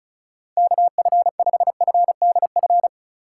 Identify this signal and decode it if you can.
Morse code, heard as a keyed tone, about 36 wpm, 700 Hz. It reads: KF5FDF
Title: USA Call Signs - 6 Characters - 2x Character Spacing - 35wpm